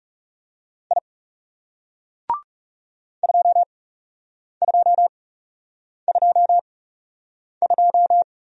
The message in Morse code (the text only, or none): I 2222